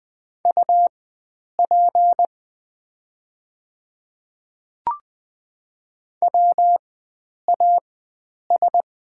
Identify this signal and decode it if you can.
Morse code, heard as a keyed tone, about 20 wpm, spaced as for 9 wpm, 700 Hz. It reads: UP WAS
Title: 4x Character Spacing - Top 100 Words - 20wpm